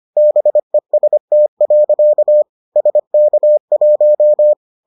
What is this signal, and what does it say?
Morse code, heard as a keyed tone, about 25 wpm, 600 Hz. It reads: BEST. SK1